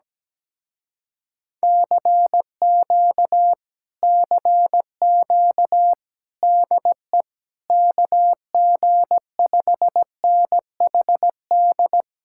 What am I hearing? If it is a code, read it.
Morse code, heard as a keyed tone, about 17 wpm, 700 Hz. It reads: CQ CQ DE KG5NHD
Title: Speed-Racing Common QSO Phrases Encore - 17wpm